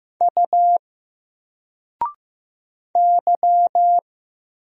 Morse code, heard as a keyed tone, about 15 wpm, 700 Hz. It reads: U Y
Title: Single Character (Letter:Number:Punctuation:Prosign) 15wpm